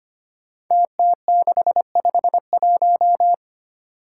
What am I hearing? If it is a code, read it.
Morse code, heard as a keyed tone, about 25 wpm, 700 Hz. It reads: TT651
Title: Five Challenging Letter-Number Combinations 25wpm